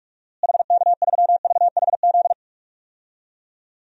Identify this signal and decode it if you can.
Morse code, heard as a keyed tone, about 46 wpm, 700 Hz. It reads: HK3VHZ
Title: DXCC-100 Call Signs - 6 Characters - Repeated 2x - 45wpm